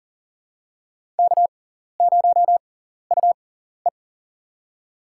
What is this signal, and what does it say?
Morse code, heard as a keyed tone, about 40 wpm, spaced as for 14 wpm, 700 Hz. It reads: K0UE
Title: USA Call Signs - 6x Character Spacing - 40wpm